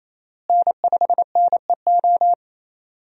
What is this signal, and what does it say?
Morse code, heard as a keyed tone, about 28 wpm, 700 Hz. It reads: N5NEO
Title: USA Call Signs - 5 Characters - Repeated 2x - 28wpm